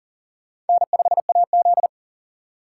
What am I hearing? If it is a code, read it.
Morse code, heard as a keyed tone, about 51 wpm, 700 Hz. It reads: N5AZ